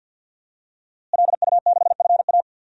Morse code, heard as a keyed tone, about 52 wpm, 700 Hz. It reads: LU6FA
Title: DXCC-100 Call Signs - 5 Characters - Repeated 2x - 50wpm